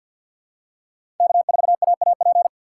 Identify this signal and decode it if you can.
Morse code, heard as a keyed tone, about 50 wpm, 700 Hz. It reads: K4AAP
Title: USA Call Signs - 5 Characters - 6x Character Spacing - 50wpm